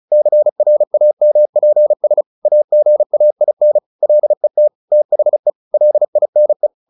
Morse code, heard as a keyed tone, about 36 wpm, 600 Hz. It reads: CRAMPS AGAIN LET THE LINE
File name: the old man and the sea 2 35wpm